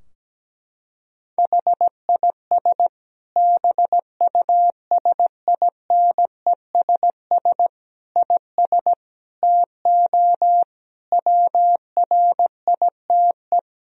Morse code, heard as a keyed tone, about 17 wpm, 700 Hz. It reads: HIS BUSINESS IS TO WRITE
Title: Sentences from Top 1000 Words - Review Part 3 - 17wpm